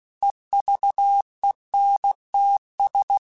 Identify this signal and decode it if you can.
Morse code, heard as a keyed tone, about 16 wpm, 800 Hz. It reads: EVENTS